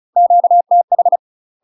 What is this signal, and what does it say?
Morse code, heard as a keyed tone, about 34 wpm, 700 Hz. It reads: QTH
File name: UK MASTER QSOs TRAINING / 35WPM / UK MASTER QSOs 0015